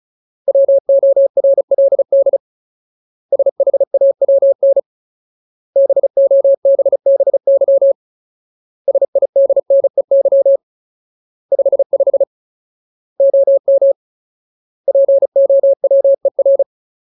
Morse code, heard as a keyed tone, about 34 wpm, 550 Hz. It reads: WORLD SHAWN BOBBY SIDNEY 55 OM POWER
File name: Ultimate Word List / 35WPM Ultimate